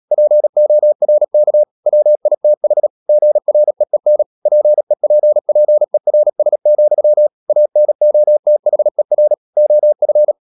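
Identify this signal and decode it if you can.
Morse code, heard as a keyed tone, about 37 wpm, 600 Hz. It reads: PORK WITH GREEN PEPPERS, ANOTHER OF
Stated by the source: Morse code practice at 35–39 wpm sending English text of Ken Follett's novel Never, from CW med Gnister.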